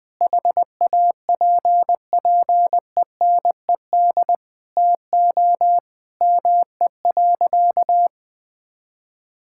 Morse code, heard as a keyed tone, about 20 wpm, 700 Hz. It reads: HAPPENED TO ME.
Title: Sentences with 8-Word Phrases - Spoken Beginning - 20wpm